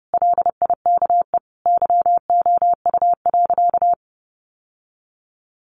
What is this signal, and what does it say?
Morse code, heard as a keyed tone, about 30 wpm, 700 Hz. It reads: LIKE YOU.
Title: Sentences from Top 1100 Words 30wpm